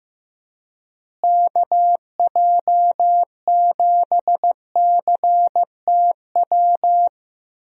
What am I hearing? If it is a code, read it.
Morse code, heard as a keyed tone, about 15 wpm, 700 Hz. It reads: KJ7CTW
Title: USA Call Signs - 6 Characters - Repeated 3x - 15wpm